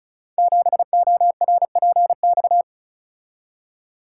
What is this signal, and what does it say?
Morse code, heard as a keyed tone, about 36 wpm, 700 Hz. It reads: 7ORPX
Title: Code Groups - 5 Letter:Numbers - 35wpm